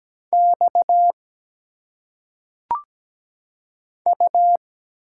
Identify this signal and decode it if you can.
Morse code, heard as a keyed tone, about 17 wpm, 700 Hz. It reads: X U
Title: Single Letters - Phonetic Pronunciation 17wpm